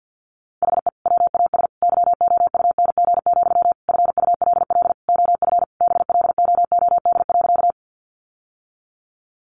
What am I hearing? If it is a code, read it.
Morse code, heard as a keyed tone, about 50 wpm, 700 Hz. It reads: HE WAS YOUNG, FULL OF BLOOD.
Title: Sentences from Top 1100 Words 50wpm